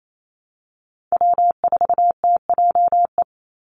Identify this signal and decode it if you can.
Morse code, heard as a keyed tone, about 28 wpm, 700 Hz. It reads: W4TJE
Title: Call Signs Repeated 2x - 28wpm